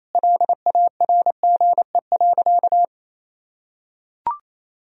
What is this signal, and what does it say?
Morse code, heard as a keyed tone, about 28 wpm, 700 Hz. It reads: LARGE.
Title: Sentences from Top 1000 Words - Review Part 1 - 28wpm